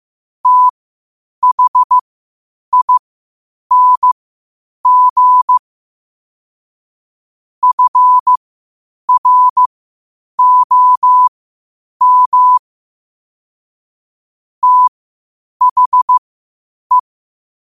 Morse code, heard as a keyed tone, about 15 wpm, spaced as for 8 wpm, 1000 Hz. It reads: THING FROM THE
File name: qotdfiles / 8wpm